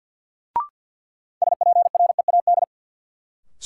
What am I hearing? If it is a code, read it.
Morse code, heard as a keyed tone, about 51 wpm, 700 Hz. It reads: SPREAD
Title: Sentences from Top 1200 Words - New Word Drill - 50wpm